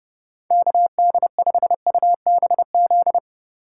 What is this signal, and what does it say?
Morse code, heard as a keyed tone, about 30 wpm, 700 Hz. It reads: KD5UBZ
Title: USA Call Signs - 3x Character Spacing - 30wpm